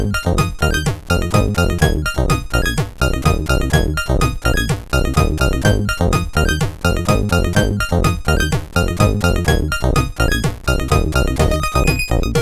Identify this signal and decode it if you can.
There is no code.